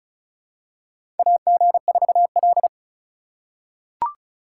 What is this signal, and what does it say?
Morse code, heard as a keyed tone, about 36 wpm, 700 Hz. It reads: AG4L
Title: USA Call Signs - 4 Characters - 4x Character Spacing - 35wpm